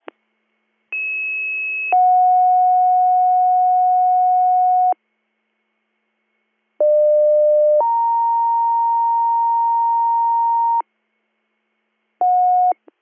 Two-tone page, heard as a one-second tone, then a three-second tone, about 2600 then 730 Hz, about 600 then 930 Hz. It is heard from 0.9 to 10.8 s.